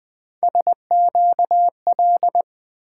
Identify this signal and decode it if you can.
Morse code, heard as a keyed tone, about 20 wpm, 700 Hz. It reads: SQL
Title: Three Letter-Number Combinations - 20wpm